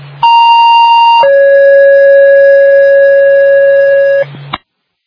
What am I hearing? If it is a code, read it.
Two-tone page, heard as a one-second tone, then a three-second tone, about 930 then 550 Hz. It is heard from 0.2 to 4.2 s.
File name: Northville